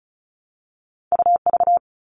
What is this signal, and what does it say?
Morse code, heard as a keyed tone, about 36 wpm, 700 Hz. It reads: UV
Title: All Cut Numbers - 2-Digit 35wpm